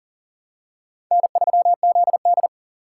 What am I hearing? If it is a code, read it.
Morse code, heard as a keyed tone, about 40 wpm, 700 Hz. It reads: N3ZD
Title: USA Call Signs - 3x Character Spacing - 40wpm